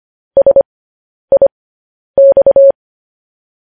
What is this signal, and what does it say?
Morse code, heard as a keyed tone, about 25 wpm, spaced as for 10 wpm, 550 Hz. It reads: SIX